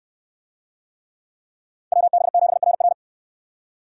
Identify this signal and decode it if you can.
Morse code, heard as a keyed tone, about 63 wpm, 700 Hz. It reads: WB7RR